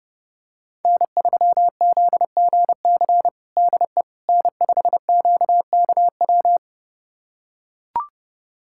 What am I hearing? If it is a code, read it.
Morse code, heard as a keyed tone, about 30 wpm, 700 Hz. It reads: N3ZGC DE N5QKW
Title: USA Callsign DE Callsign - 5 Characters - 30wpm